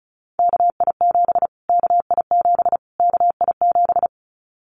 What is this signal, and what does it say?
Morse code, heard as a keyed tone, about 34 wpm, 700 Hz. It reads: KI7 KI7 KI7
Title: Callsign Prefixes for Top 25 DXCC - Repeated 3x - 35wpm